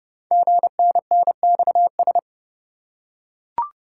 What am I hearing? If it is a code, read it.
Morse code, heard as a keyed tone, about 30 wpm, 700 Hz. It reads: GNNXS
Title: Code Groups - 5 Letters - 30wpm